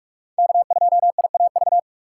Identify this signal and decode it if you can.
Morse code, heard as a keyed tone, about 46 wpm, 700 Hz. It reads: K2IAV
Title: USA Call Signs - 5 Characters - 3x Character Spacing - 45wpm